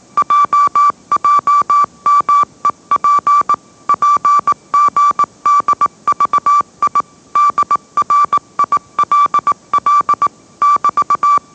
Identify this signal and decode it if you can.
Morse code, heard as a keyed tone, about 19 wpm, 1160 Hz. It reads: JJMEP PGDVI DRILL <BT>
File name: mrx / mrx082 20wpm